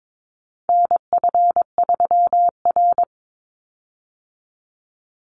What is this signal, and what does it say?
Morse code, heard as a keyed tone, about 22 wpm, 700 Hz. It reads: NF3R
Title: Call Signs - 22wpm